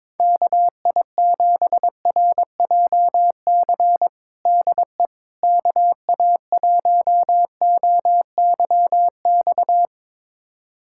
Morse code, heard as a keyed tone, about 22 wpm, 700 Hz. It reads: KI7RJC DE KA1OYX
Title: USA Callsign DE Callsign - 6 Characters - 22wpm